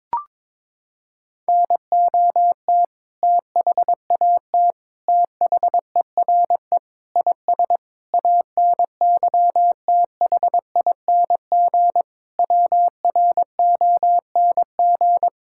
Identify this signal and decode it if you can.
Morse code, heard as a keyed tone, about 22 wpm, 700 Hz. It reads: NOT THAT THERE IS ANYTHING WRONG
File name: Sentences with 7-Word Phrases - 22wpm